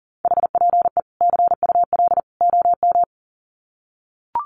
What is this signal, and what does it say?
Morse code, heard as a keyed tone, about 40 wpm, 700 Hz. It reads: HPE CUL OM
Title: Common QSO Phrases 40wpm